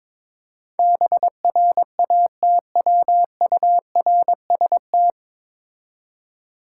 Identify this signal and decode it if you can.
Morse code, heard as a keyed tone, about 22 wpm, 700 Hz. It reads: BRATWURST
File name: Difficult Words with Hints - 22wpm